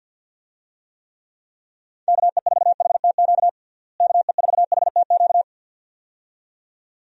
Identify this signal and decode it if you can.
Morse code, heard as a keyed tone, about 51 wpm, 700 Hz. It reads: KE4HTQ KE4HTQ
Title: USA Call Signs - 6 Characters - Repeated 2x - 50wpm